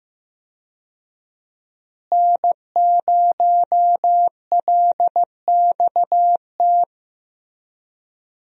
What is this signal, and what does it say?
Morse code, heard as a keyed tone, about 15 wpm, 700 Hz. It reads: N0LXT
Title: USA Call Signs - 5 Characters - Repeated 3x - 15wpm